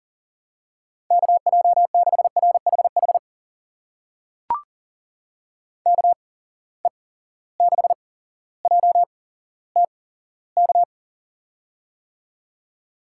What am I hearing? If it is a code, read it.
Morse code, heard as a keyed tone, about 38 wpm, spaced as for 17 wpm, 700 Hz. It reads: KJ6RHH KE6JTK
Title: USA Call Signs - 6 Characters - 8x Character Spacing - 40wpm